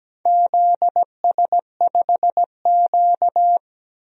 Morse code, heard as a keyed tone, about 17 wpm, 700 Hz. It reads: ZS5Q
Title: DXCC-100 Call Signs - 4 Characters - Repeated 2x - 17wpm